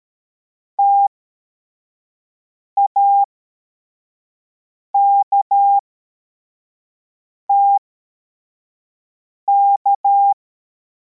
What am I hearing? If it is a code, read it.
Morse code, heard as a keyed tone, about 13 wpm, spaced as for 4 wpm, 800 Hz. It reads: TAKTK